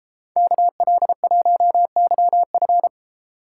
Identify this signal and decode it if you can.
Morse code, heard as a keyed tone, about 33 wpm, 700 Hz. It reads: KL1YF
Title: Speed Racing - USA Call Signs 28wpm